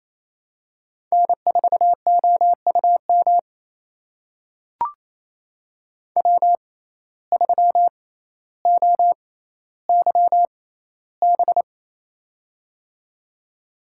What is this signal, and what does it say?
Morse code, heard as a keyed tone, about 28 wpm, spaced as for 14 wpm, 700 Hz. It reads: N4OUM W3OYB
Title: USA Call Signs - 5 Characters - 6x Character Spacing - 28wpm